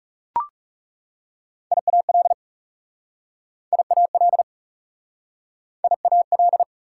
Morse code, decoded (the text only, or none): IAL IAL IAL